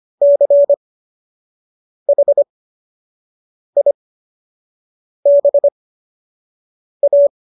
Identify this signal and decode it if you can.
Morse code, heard as a keyed tone, about 25 wpm, spaced as for 6 wpm, 570 Hz. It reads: CHIBA